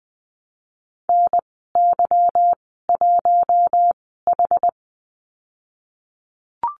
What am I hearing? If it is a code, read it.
Morse code, heard as a keyed tone, about 20 wpm, spaced as for 15 wpm, 700 Hz. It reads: NY1H
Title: 193 Lesson 31g - Letter v - Call Signs 20wpm